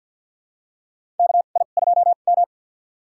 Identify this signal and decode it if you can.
Morse code, heard as a keyed tone, about 49 wpm, spaced as for 36 wpm, 700 Hz. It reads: KI2M
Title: USA Call Signs - 2x Character Spacing - 50wpm